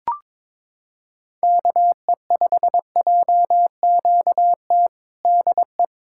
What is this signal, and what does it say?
Morse code, heard as a keyed tone, about 22 wpm, 700 Hz. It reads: KE5JQT DE
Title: K1USN SST QSO Practice 22wpm